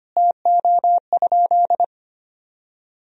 Morse code, heard as a keyed tone, about 25 wpm, 700 Hz. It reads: TO?
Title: Sentences from Top 1000 Words - Review Part 4 - 25wpm